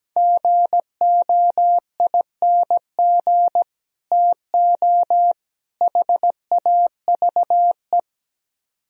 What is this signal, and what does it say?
Morse code, heard as a keyed tone, about 17 wpm, 700 Hz. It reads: GOING TO HAVE
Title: Speed-Racing - Sets of 3 Words 17wpm